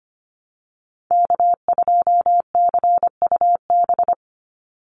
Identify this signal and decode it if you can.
Morse code, heard as a keyed tone, about 25 wpm, 700 Hz. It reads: K2CUB
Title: Call Signs - 25wpm